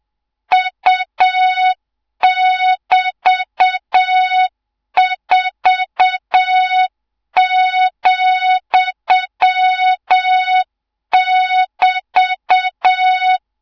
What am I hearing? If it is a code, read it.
Morse code, heard as a keyed tone, about 7 wpm, 3700 Hz, with noise. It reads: U<BT>4,<BT>
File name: Morze-HA2MN / hang / 24 GYAKORLAT